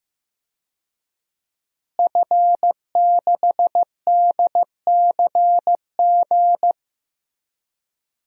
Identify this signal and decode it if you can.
Morse code, heard as a keyed tone, about 15 wpm, 700 Hz. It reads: F6DCG
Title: DXCC-100 Call Signs - 5 Characters - Repeated 2x - 15wpm